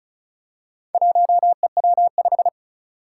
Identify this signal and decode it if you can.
Morse code, heard as a keyed tone, about 36 wpm, 700 Hz. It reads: 1EW5
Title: Four Challenging Letter-Number Combinations 35wpm